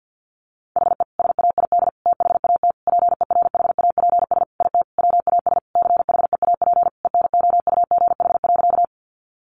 Morse code, heard as a keyed tone, about 50 wpm, 700 Hz. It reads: HE SAID THAT PERHAPS IT WAS CHEAP ENOUGH.
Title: Sentences from Top 1200 Words 50wpm